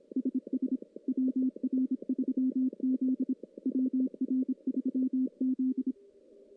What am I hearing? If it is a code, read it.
Morse code, heard as a keyed tone, about 26 wpm, 260 Hz. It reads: SS WR3Z WR3Z
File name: contests / CW-SS-k5tr- -1600